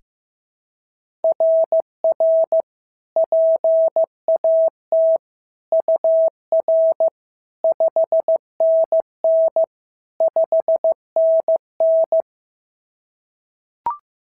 Morse code, heard as a keyed tone, about 14 wpm, 650 Hz. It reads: RR PAT UR 5NN 5NN
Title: SOTA QSO Practice 15wpm